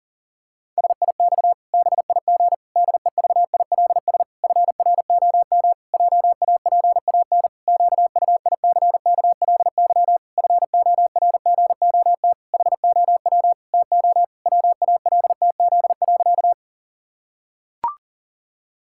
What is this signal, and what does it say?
Morse code, heard as a keyed tone, about 40 wpm, 700 Hz. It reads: SIX BIG DEVILS FROM JAPAN QUICKLY FORGOT HOW TO WALTZ.
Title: Pangram Challenge v2 - 40wpm